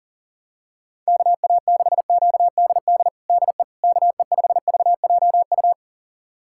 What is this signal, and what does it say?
Morse code, heard as a keyed tone, about 40 wpm, 700 Hz. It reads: KA6QDD DE KE5VJU
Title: USA Callsign DE Callsign 40wpm